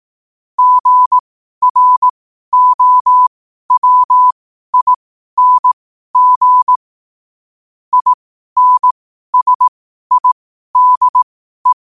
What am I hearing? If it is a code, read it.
Morse code, heard as a keyed tone, about 18 wpm, spaced as for 12 wpm, 1000 Hz. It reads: GROWING INSIDE
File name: qotdfiles / 12wpm